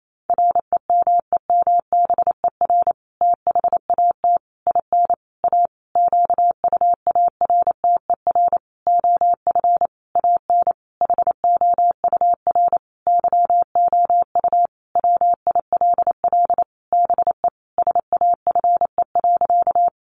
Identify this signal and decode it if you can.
Morse code, heard as a keyed tone, about 28 wpm, 700 Hz. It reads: REMEMBER THAT IN A QUARTER OF AN HOUR YOU WILL BE SAFE.
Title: Sentences from Top 1300 Words 28wpm